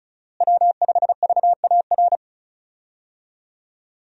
Morse code, heard as a keyed tone, about 36 wpm, 700 Hz. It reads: W5VAR
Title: USA Call Signs - 5 Characters - Repeated 3x - 35wpm